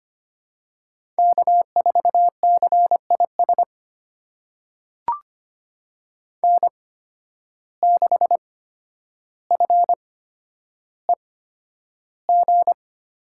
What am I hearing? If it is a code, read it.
Morse code, heard as a keyed tone, about 25 wpm, spaced as for 11 wpm, 700 Hz. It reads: K4CIS N6FEG
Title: USA Call Signs - 5 Characters - 8x Character Spacing - 25wpm